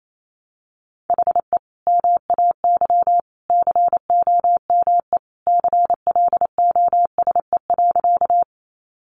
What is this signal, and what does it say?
Morse code, heard as a keyed tone, about 28 wpm, 700 Hz. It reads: HE MAY COME CLOSE.
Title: Sentences from Top 300 Words 28wpm